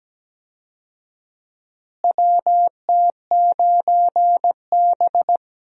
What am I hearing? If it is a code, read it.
Morse code, heard as a keyed tone, about 17 wpm, 700 Hz. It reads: WT9B
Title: USA Call Signs - 4 Characters - Repeated 3x - 17wpm